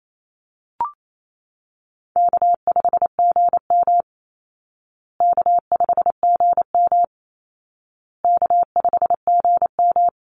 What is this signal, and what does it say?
Morse code, heard as a keyed tone, about 28 wpm, 700 Hz. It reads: K5GM K5GM K5GM
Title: Easy Call Signs Repeated 3x - 28wpm